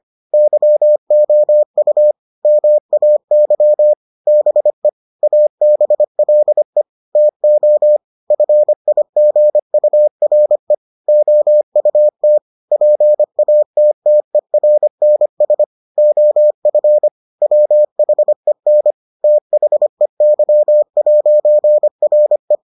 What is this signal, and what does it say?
Morse code, heard as a keyed tone, about 25 wpm, 600 Hz. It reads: YOU MAY BE ABLE TO FIGURE OUT PATTERNS OF WHEN THEY'RE